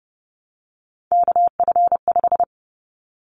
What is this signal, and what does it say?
Morse code, heard as a keyed tone, about 30 wpm, 700 Hz. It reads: KF5
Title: Callsign Prefixes for Top 25 DXCC - Repeated 3x - 30wpm